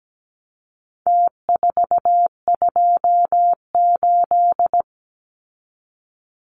Morse code, heard as a keyed tone, about 17 wpm, 700 Hz. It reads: T428
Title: Numbers - Contest Style - 4-Digit 17wpm